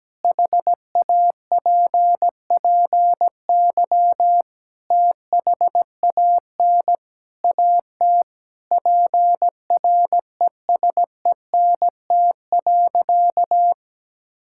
Morse code, heard as a keyed tone, about 17 wpm, 700 Hz. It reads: HAPPY THAN AT PRESENT.
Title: Sentences from Top 1000 Words - Review Part 5 - 17wpm